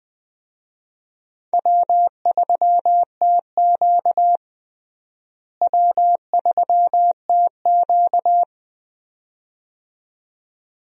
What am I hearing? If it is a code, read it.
Morse code, heard as a keyed tone, about 20 wpm, 700 Hz. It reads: W3TQ W3TQ
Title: USA Call Signs - 4 Characters - Repeated 2x - 20wpm